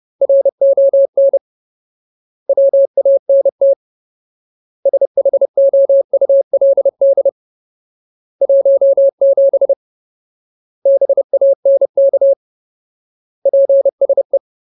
Morse code, heard as a keyed tone, about 30 wpm, 550 Hz. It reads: RON WANT SHOULD 17 BANK PSE
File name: Ultimate Word List / 30WPM Ultimate